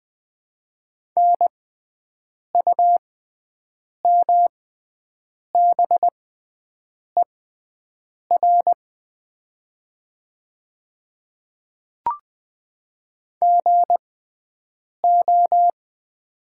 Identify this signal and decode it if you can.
Morse code, heard as a keyed tone, about 20 wpm, spaced as for 7 wpm, 700 Hz. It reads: NUMBER GO
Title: 6x Character Spacing - Top 100 Words - 20wpm